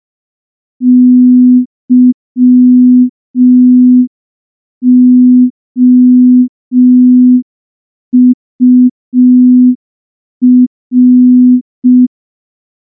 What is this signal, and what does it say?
Morse code, heard as a keyed tone, about 5 wpm, 250 Hz. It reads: YOUR
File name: morse / morse-training-05WPM